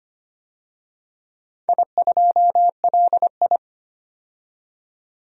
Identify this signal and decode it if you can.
Morse code, heard as a keyed tone, about 25 wpm, 700 Hz. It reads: I2LI
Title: DXCC-100 Call Signs - 4 Characters - 25wpm